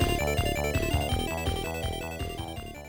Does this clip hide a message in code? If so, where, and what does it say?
No.